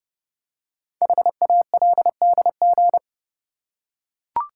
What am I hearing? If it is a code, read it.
Morse code, heard as a keyed tone, about 29 wpm, 700 Hz. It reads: HALDG
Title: Code Groups - 5 Letters - 30wpm